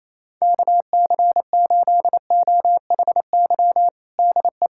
Morse code, heard as a keyed tone, about 28 wpm, 700 Hz. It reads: KC8OHY DE
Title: USA Callsign DE Callsign 28wpm